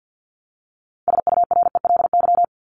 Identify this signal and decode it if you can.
Morse code, heard as a keyed tone, about 50 wpm, 700 Hz. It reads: SURELY
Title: LY Suffix - Top 100 Words with Base Words 50wpm